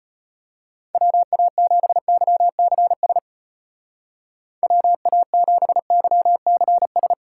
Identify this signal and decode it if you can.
Morse code, heard as a keyed tone, about 33 wpm, 700 Hz. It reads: WA7YCS WA7YCS